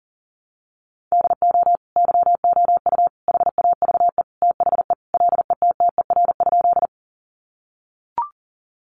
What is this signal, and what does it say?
Morse code, heard as a keyed tone, about 40 wpm, 700 Hz. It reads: DO YOU HAVE THE LETTER?
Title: Sentences from Top 300 Words 40wpm